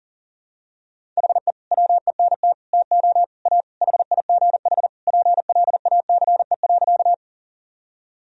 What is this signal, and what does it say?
Morse code, heard as a keyed tone, about 40 wpm, 700 Hz. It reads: HE WENT TO A HIGH PLACE.
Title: Sentences from Top 1000 Words - Review Part 1 - 40wpm